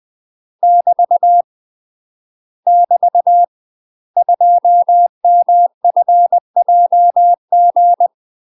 Morse code, heard as a keyed tone, about 20 wpm, 700 Hz. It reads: <BT> <BT> 2MFJG
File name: UK MASTER QSOs TRAINING/20WPM